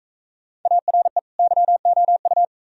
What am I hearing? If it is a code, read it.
Morse code, heard as a keyed tone, about 41 wpm, 700 Hz. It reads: ARE YOU